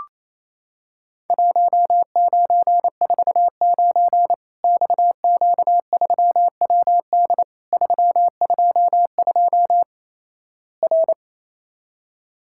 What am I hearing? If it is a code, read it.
Morse code, heard as a keyed tone, about 28 wpm, 700 Hz. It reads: 1949 XQ3WD 322 R
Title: Worked All Europe DX Contest - QTC Exchange - 28wpm